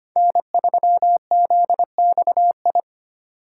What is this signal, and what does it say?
Morse code, heard as a keyed tone, about 25 wpm, 700 Hz. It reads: N3ZXI
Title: USA Call Signs - 5 Characters - 3x Character Spacing - 25wpm